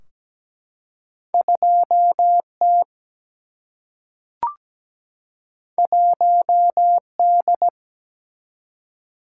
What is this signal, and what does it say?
Morse code, heard as a keyed tone, about 17 wpm, 700 Hz. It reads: 2 T 1 D
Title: Single-Letter Single-Number 17wpm